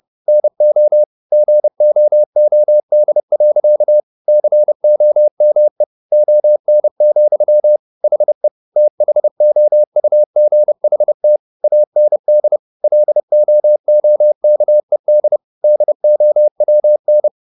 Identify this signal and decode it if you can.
Morse code, heard as a keyed tone, about 30 wpm, 600 Hz. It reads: NO GOOD. COME ON, HE THOUGHT AND LOOKED DOWN
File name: the old man and the sea / the old man and the sea 1 30wpm